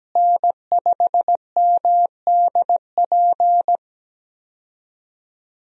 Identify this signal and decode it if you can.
Morse code, heard as a keyed tone, about 17 wpm, 700 Hz. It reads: N5MDP
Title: USA Call Signs - 5 Characters - Repeated 3x - 17wpm